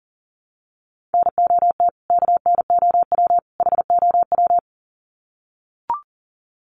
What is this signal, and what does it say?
Morse code, heard as a keyed tone, about 40 wpm, 700 Hz. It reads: NOT KNOW HOW
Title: Sets of 3 Words 40wpm